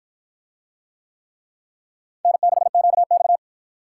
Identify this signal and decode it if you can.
Morse code, heard as a keyed tone, about 52 wpm, 700 Hz. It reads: N6QX